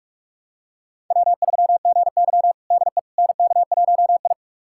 Morse code, heard as a keyed tone, about 46 wpm, 700 Hz. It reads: W3GY DE NK1I
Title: USA Callsign DE Callsign - 4 Characters - 45wpm